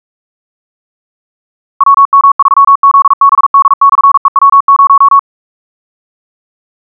Morse code, heard as a keyed tone, about 42 wpm, 1110 Hz. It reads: WM3ZBD/EW0
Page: SampleHST1.wav <-- 158cpm (=47wpm(paris)) 1111Hz sinusoidal waveform